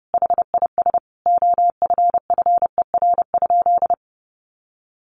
Morse code, heard as a keyed tone, about 28 wpm, 700 Hz. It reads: HIS OFFER?
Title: Sentences from Top 1300 Words 30wpm